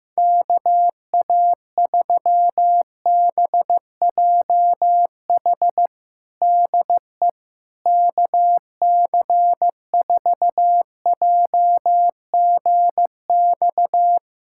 Morse code, heard as a keyed tone, about 15 wpm, 700 Hz. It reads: KA3BJH DE KC4JGX
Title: USA Callsign DE Callsign - 6 Characters - 15wpm